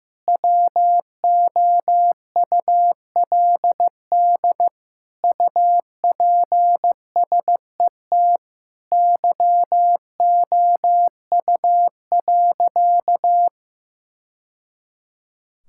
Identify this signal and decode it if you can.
Morse code, heard as a keyed tone, about 15 wpm, 700 Hz. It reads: WOULD UPSET YOU.
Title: Sentences from Top 1000 Words - Review Part 5 - 15wpm